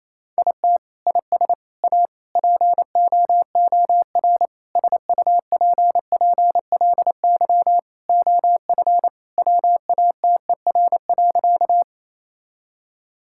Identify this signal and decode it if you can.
Morse code, heard as a keyed tone, about 28 wpm, 700 Hz. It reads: IT IS A POOR SUPPLY OF WATER.
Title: Sentences from Top 1000 Words - Review Part 5 - 28wpm